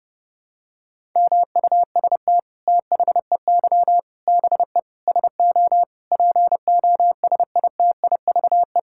Morse code, heard as a keyed tone, about 30 wpm, 700 Hz. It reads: MUST THEY BE SO POSITIVE
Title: Sentences from Top 1000 Words - Review Part 5 - 30wpm